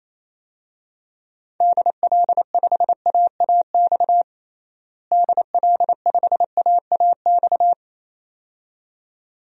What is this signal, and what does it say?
Morse code, heard as a keyed tone, about 28 wpm, 700 Hz. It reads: DL5AAX DL5AAX
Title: DXCC-100 Call Signs - 6 Characters - Repeated 2x - 28wpm